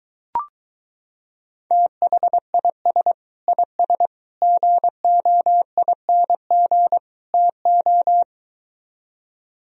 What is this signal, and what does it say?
Morse code, heard as a keyed tone, about 23 wpm, 700 Hz. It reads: THIS IS GOING TO